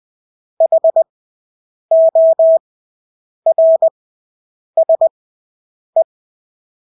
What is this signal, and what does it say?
Morse code, heard as a keyed tone, about 20 wpm, spaced as for 8 wpm, 650 Hz. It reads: HORSE
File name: Engelske ord 20/8 wpm